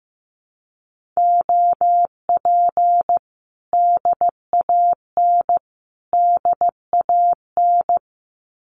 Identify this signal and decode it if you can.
Morse code, heard as a keyed tone, about 15 wpm, 700 Hz. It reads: OP DAN DAN
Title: Common QSO Phrases Encore 15wpm